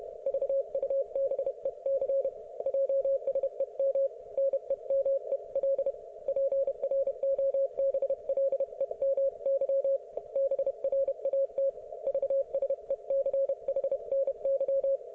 Morse code, heard as a keyed tone, about 31 wpm, 530 Hz. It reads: VUBEC JSEM NEMEL PROBLEMY BRAT VSECHNY